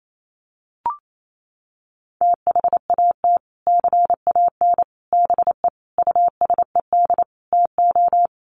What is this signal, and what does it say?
Morse code, heard as a keyed tone, about 28 wpm, 700 Hz. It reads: THAT CAN BE USED TO
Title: Sets of 5 Words Encore 28wpm